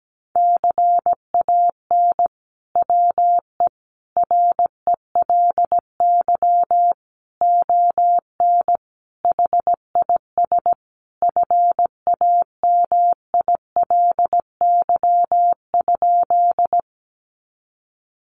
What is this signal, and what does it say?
Morse code, heard as a keyed tone, about 17 wpm, 700 Hz. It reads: CAN WE RELY ON HIS FAMILY?
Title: Sentences from Top 1200 Words 17wpm